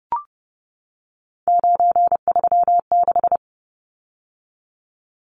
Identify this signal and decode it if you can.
Morse code, heard as a keyed tone, about 30 wpm, 700 Hz. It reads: 936
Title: Numbers - 3-Digit 30wpm